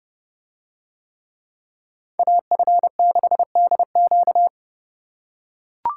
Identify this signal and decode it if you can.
Morse code, heard as a keyed tone, about 29 wpm, 700 Hz. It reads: AF6DQ